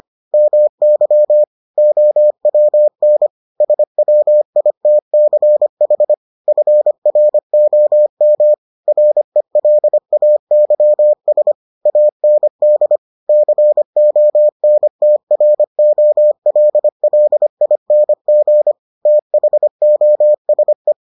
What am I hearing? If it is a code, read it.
Morse code, heard as a keyed tone, about 25 wpm, 600 Hz. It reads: MY OWN SWITCH FROM RELAYS AND CONTROLLING THOSE